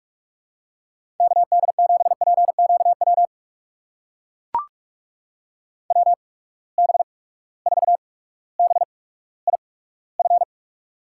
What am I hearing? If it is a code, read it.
Morse code, heard as a keyed tone, about 44 wpm, spaced as for 19 wpm, 700 Hz. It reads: KD7PQW WB4BIF
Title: USA Call Signs - 6 Characters - 8x Character Spacing - 45wpm